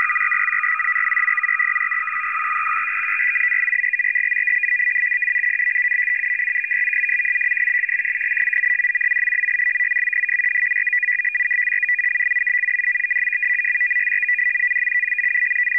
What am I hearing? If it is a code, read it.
Two-tone teleprinter signal, about 450 Hz shift.
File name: POL- -0900and0905utc-fsk100-500